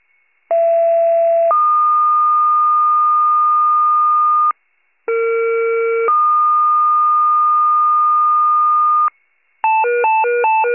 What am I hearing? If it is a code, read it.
Two-tone page, heard as a one-second tone, then a three-second tone, about 670 then 1220 Hz, about 460 then 1220 Hz. It is heard from 0.5 to 9.1 s.
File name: wcalert